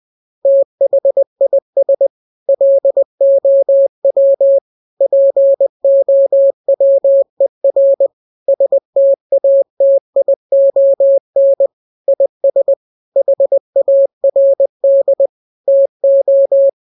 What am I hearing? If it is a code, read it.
Morse code, heard as a keyed tone, about 20 wpm, 550 Hz. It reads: THIS LOW POWER STATION IS HARD TO
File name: jokes / joke 4 20 0000